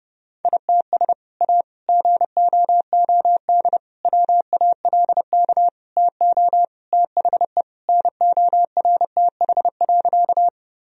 Morse code, heard as a keyed tone, about 30 wpm, 700 Hz. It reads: ITS A GOOD WALK TO THE NORTH.
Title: Sentences from Top 1000 Words - Review Part 1 - 30wpm